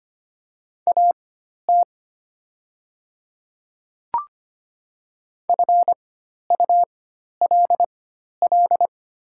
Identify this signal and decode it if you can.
Morse code, heard as a keyed tone, about 25 wpm, spaced as for 12 wpm, 700 Hz. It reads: AT FULL
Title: 4x Character Spacing - Top 500 Words - 25wpm